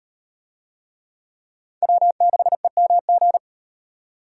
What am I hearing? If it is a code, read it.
Morse code, heard as a keyed tone, about 38 wpm, 700 Hz. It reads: W6EMG